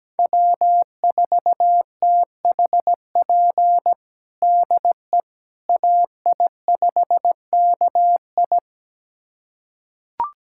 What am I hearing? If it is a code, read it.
Morse code, heard as a keyed tone, about 17 wpm, 700 Hz. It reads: W4THP DE AI5KI
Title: USA Callsign DE Callsign - 5 Characters - 17wpm